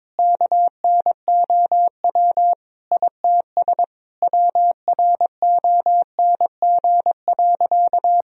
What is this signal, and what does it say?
Morse code, heard as a keyed tone, about 22 wpm, 700 Hz. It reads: KNOW ITS WRONG.
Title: Sentences from Top 1000 Words - Review Part 1 - 22wpm